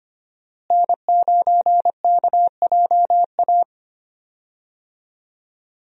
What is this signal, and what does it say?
Morse code, heard as a keyed tone, about 25 wpm, 700 Hz. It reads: N9KJA
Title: USA Call Signs - 5 Characters - Repeated 2x - 25wpm